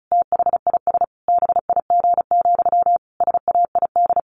Morse code, heard as a keyed tone, about 34 wpm, 700 Hz. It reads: THIS BIG, SAID
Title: A Sound Of Thunder 35wpm